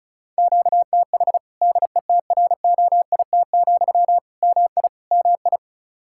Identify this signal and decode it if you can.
Morse code, heard as a keyed tone, about 36 wpm, 700 Hz. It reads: QTH DETROIT, MI MI
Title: Practice QSO - Exchange 1 - Beginning - 35wpm